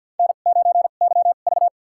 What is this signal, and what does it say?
Morse code, heard as a keyed tone, about 52 wpm, spaced as for 36 wpm, 700 Hz. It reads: N9YV
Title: USA Call Signs - 2x Character Spacing - 50wpm